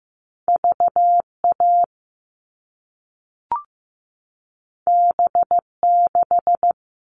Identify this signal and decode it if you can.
Morse code, heard as a keyed tone, about 15 wpm, 700 Hz. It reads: VA B6
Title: All Cut Numbers - 2-Digit 15wpm